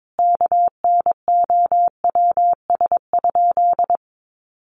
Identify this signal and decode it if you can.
Morse code, heard as a keyed tone, about 22 wpm, 700 Hz. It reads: KNOWS?
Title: A Sound Of Thunder 22wpm